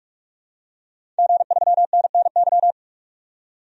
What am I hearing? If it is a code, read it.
Morse code, heard as a keyed tone, about 45 wpm, 700 Hz. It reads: G3NNY
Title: DXCC-100 Call Signs - 5 Characters - 45wpm